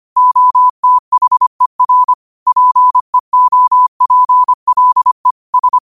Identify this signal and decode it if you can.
Morse code, heard as a keyed tone, about 25 wpm, 1000 Hz. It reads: OTHER PEOPLE S
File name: qotdfiles / 25wpm